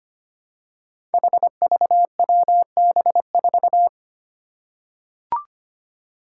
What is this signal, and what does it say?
Morse code, heard as a keyed tone, about 25 wpm, 700 Hz. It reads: HVWB4
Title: Five Challenging Letter-Number Combinations 25wpm